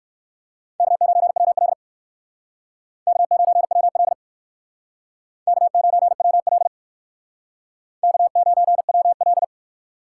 Morse code, heard as a keyed tone, about 56 wpm, 700 Hz. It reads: K9WL K9WL K9WL K9WL
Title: Speed Racing - USA Call Signs 45wpm